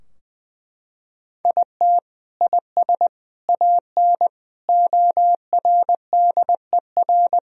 Morse code, heard as a keyed tone, about 20 wpm, 700 Hz. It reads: IT IS AN ORDER
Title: Sentences from Top 1000 Words - Review Part 2 - 20wpm